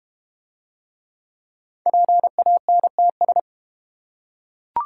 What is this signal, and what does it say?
Morse code, heard as a keyed tone, about 32 wpm, 700 Hz. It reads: PANTS